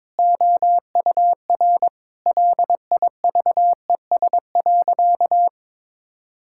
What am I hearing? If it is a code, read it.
Morse code, heard as a keyed tone, about 22 wpm, 700 Hz. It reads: OUR LIVES.
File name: practice / 22wpm / Sentences with 2-Word Phrases - 22wpm